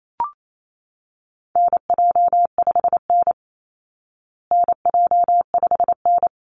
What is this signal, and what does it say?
Morse code, heard as a keyed tone, about 28 wpm, 700 Hz. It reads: NJ5N NJ5N
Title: Call Signs Repeated 3x - 28wpm